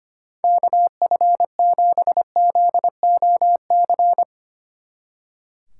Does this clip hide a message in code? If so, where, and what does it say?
Yes: Morse code from 0.4 to 4.2 s, heard as a keyed tone, about 32 wpm, 700 Hz. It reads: KF7ZOC